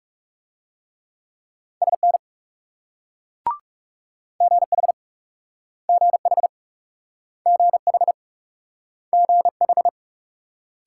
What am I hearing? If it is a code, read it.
Morse code, heard as a keyed tone, about 38 wpm, 700 Hz. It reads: SN GH GH GH GH
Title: Speed-Racing Multi-letter Phonemes - 30wpm